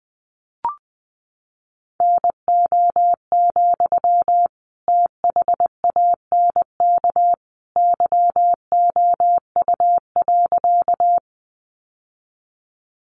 Morse code, heard as a keyed tone, about 20 wpm, 700 Hz. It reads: NO, THANK YOU.
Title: Sentences from Top 600 Words 20wpm